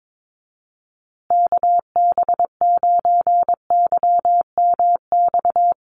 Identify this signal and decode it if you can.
Morse code, heard as a keyed tone, about 22 wpm, 700 Hz. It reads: KB9YMX
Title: USA Call Signs Repeated 3x 22wpm